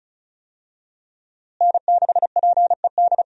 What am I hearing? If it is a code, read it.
Morse code, heard as a keyed tone, about 36 wpm, 700 Hz. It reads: N6PED
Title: USA Call Signs - 5 Characters - Repeated 2x - 35wpm